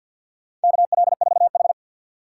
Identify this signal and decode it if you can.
Morse code, heard as a keyed tone, about 51 wpm, 700 Hz. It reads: KL4H
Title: USA Call Signs - 4 Characters - 2x Character Spacing - 50wpm